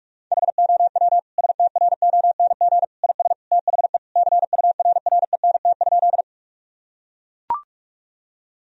Morse code, heard as a keyed tone, about 43 wpm, 700 Hz. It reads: HOW STRONG IS THE CURRENT?
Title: Sentences from Top 1000 Words - Review Part 4 - 45wpm